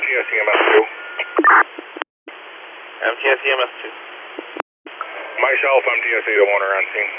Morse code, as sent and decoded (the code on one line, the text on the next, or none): none